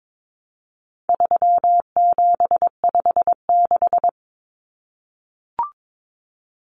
Morse code, heard as a keyed tone, about 22 wpm, 700 Hz. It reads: 3756
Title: Common Cut Numbers - 4-Digit 22wpm